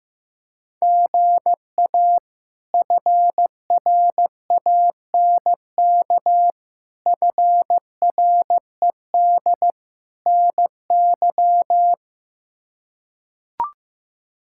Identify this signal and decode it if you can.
Morse code, heard as a keyed tone, about 15 wpm, 700 Hz. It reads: GA FRANK FRED NY
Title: K1USN SST QSO Practice - 15wpm